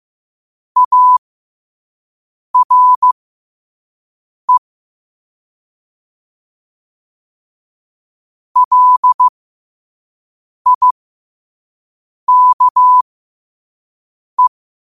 Morse code, heard as a keyed tone, about 15 wpm, spaced as for 5 wpm, 1000 Hz. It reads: ARE LIKE
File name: qotdfiles / 5wpm